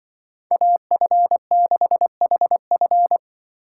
Morse code, heard as a keyed tone, about 24 wpm, 700 Hz. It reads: AF6HF
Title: Speed Racing - USA 2x2 Call Signs 20wpm